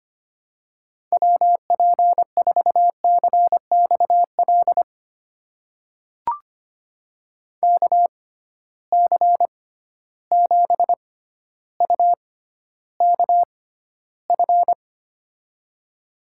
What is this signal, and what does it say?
Morse code, heard as a keyed tone, about 25 wpm, spaced as for 13 wpm, 700 Hz. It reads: WP4CXL KC7UKF
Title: USA Call Signs - 6 Characters - 6x Character Spacing - 25wpm